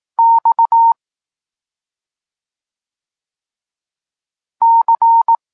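Morse code, heard as a keyed tone, about 18 wpm, 920 Hz. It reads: X C
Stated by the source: Polish Morse lesson teaching the letter C, from SP9MOA Niepołomicki Klub Krótkofalowców.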